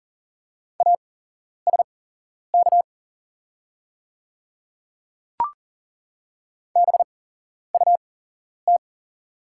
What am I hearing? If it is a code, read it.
Morse code, heard as a keyed tone, about 40 wpm, spaced as for 11 wpm, 700 Hz. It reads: ASK BUT
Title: 8x Character Spacing - Top 200 Words - 40wpm